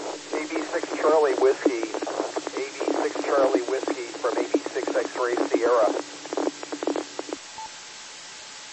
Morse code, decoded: none

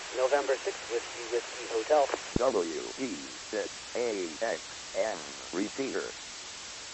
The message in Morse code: none

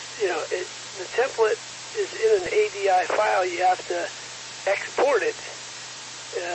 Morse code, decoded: none

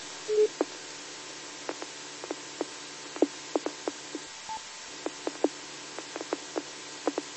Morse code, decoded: none

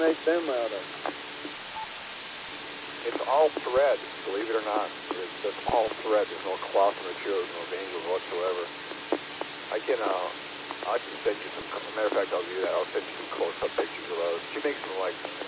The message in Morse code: none

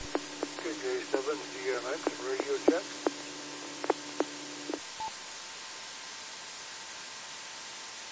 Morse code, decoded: none

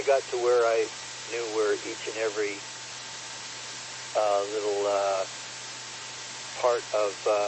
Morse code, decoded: none

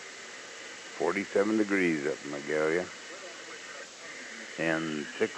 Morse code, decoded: none

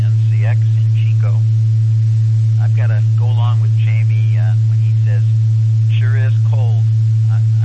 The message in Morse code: none

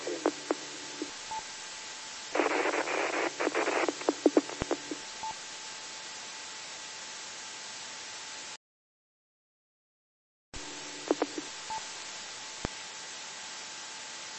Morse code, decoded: none

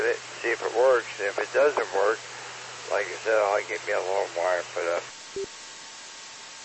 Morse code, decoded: none

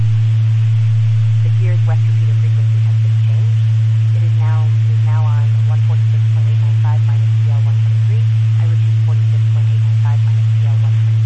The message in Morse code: none

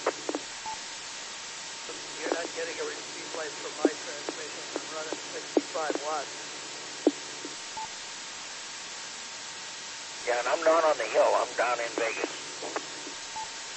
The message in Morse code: none

